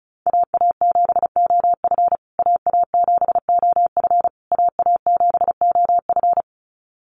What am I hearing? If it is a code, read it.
Morse code, heard as a keyed tone, about 34 wpm, 700 Hz. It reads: AA7OF AA7OF AA7OF
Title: USA Call Signs Repeated 3x 35wpm